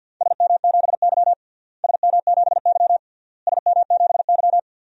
Morse code, heard as a keyed tone, about 50 wpm, 700 Hz. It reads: SM7Y SM7Y SM7Y
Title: DXCC-100 Call Signs - 4 Characters - Repeated 3x - 50wpm